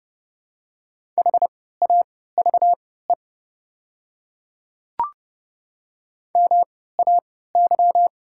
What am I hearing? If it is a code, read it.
Morse code, heard as a keyed tone, about 31 wpm, spaced as for 17 wpm, 700 Hz. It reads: HAVE MAY
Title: 3x Character Spacing - Top 200 Words - 30wpm